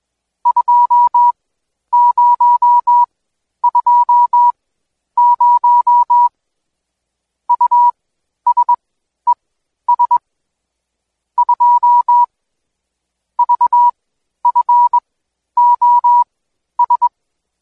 Morse code, heard as a keyed tone, about 21 wpm, spaced as for 11 wpm, 970 Hz. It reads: 2020 USES 2 VFOS
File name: Code Quick / AudioDownld2 / Seg120-Day 25-DayDrill 25